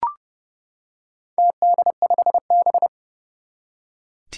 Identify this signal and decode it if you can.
Morse code, heard as a keyed tone, about 30 wpm, 700 Hz. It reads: TD5B
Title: Four Challenging Letter-Number Combinations 30wpm